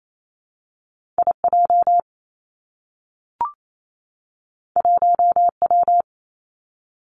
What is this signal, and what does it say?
Morse code, heard as a keyed tone, about 28 wpm, 700 Hz. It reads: I J 1 W
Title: Two Challenging Letter-Number Combinations 28wpm